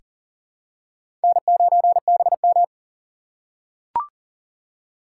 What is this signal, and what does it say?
Morse code, heard as a keyed tone, about 38 wpm, 700 Hz. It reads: N9BM
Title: USA Call Signs - 4 Characters - 3x Character Spacing - 40wpm